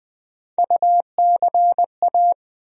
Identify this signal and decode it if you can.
Morse code, heard as a keyed tone, about 21 wpm, 700 Hz. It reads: UCA
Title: USA Call Sign Postfixes - 3 Letters 20wpm